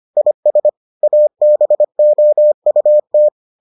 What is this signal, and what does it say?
Morse code, heard as a keyed tone, about 25 wpm, 600 Hz. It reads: IS ABOUT